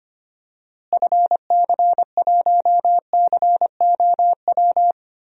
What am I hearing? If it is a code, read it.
Morse code, heard as a keyed tone, about 25 wpm, 700 Hz. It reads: FC1COW
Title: DXCC-100 Call Signs - 6 Characters - Repeated 3x - 25wpm